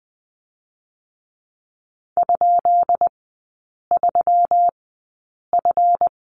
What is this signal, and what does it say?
Morse code, heard as a keyed tone, about 20 wpm, spaced as for 8 wpm, 700 Hz. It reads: ?3F
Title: 138 Lesson 23e - Letter f - Three Characters 20wpm